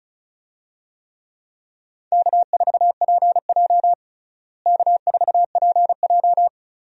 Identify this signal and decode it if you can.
Morse code, heard as a keyed tone, about 35 wpm, 700 Hz. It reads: K4PJ K4PJ
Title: USA Call Signs - 4 Characters - Repeated 3x - 35wpm